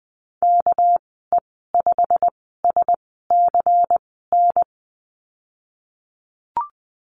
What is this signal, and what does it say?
Morse code, heard as a keyed tone, about 20 wpm, spaced as for 15 wpm, 700 Hz. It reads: KE5SCN
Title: 200 Lesson 32g - Letter k - Call Signs 20wpm